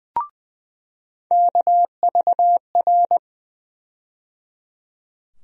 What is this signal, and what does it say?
Morse code, heard as a keyed tone, about 20 wpm, 700 Hz. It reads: KVR
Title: USA Call Sign Postfixes - 3 Letters 20wpm